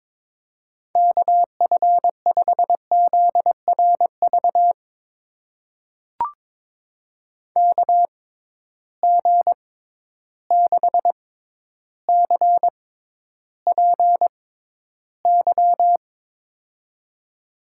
Morse code, heard as a keyed tone, about 22 wpm, spaced as for 11 wpm, 700 Hz. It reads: KF5ZRV KG6CPY
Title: USA Call Signs - 6 Characters - 6x Character Spacing - 22wpm